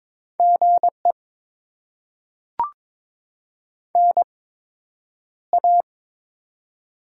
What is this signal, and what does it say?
Morse code, heard as a keyed tone, about 23 wpm, spaced as for 10 wpm, 700 Hz. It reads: GE NA
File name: Top 20 QSO Elements - 8x Character Spacing - 22wpm